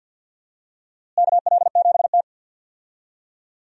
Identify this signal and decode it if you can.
Morse code, heard as a keyed tone, about 50 wpm, 700 Hz. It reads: KL7T
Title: DXCC-100 Call Signs - 4 Characters - Repeated 2x - 50wpm